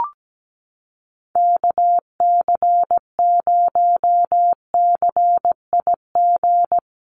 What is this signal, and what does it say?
Morse code, heard as a keyed tone, about 17 wpm, 700 Hz. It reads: KC0CIG
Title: USA Call Signs Repeated 2x 17wpm